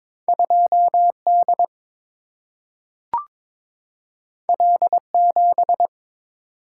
Morse code, heard as a keyed tone, about 23 wpm, 700 Hz. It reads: 2 D L 7
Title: Single-Letter Single-Number 22wpm